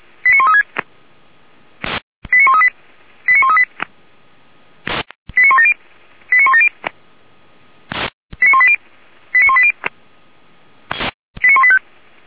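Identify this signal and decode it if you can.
Five-tone selective call (ZVEI), heard as quick runs of five tones, tones of about 70 ms. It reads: 89136 89138 89138 89170 89170 89180 89180 89185
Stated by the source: ZVEI five-tone selective call 89135 received by SDR, from audio server of Feuerwehr Orlamünde (feuerwehr-orlamuende).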